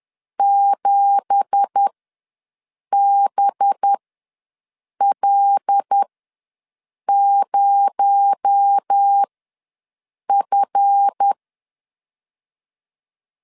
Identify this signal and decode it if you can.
Morse code, heard as a keyed tone, about 10 wpm, spaced as for 6 wpm, 800 Hz. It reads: 7BL0F
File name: LEZIONI / 9 2867 ws 10wpm 4215